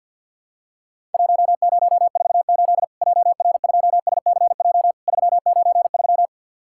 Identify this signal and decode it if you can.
Morse code, heard as a keyed tone, about 50 wpm, 700 Hz. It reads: 1048 JR2SCJ 393
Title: Worked All Europe DX Contest - QTC Exchange with Repeats - 50wpm